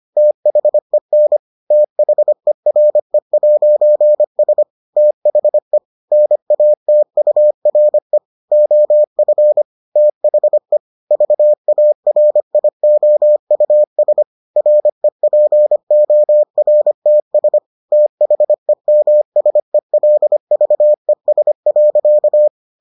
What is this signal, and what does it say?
Morse code, heard as a keyed tone, about 25 wpm, 600 Hz. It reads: THEN THERE'S THE NATURE OF THE VARIOUS REPORTS THEMSELVES.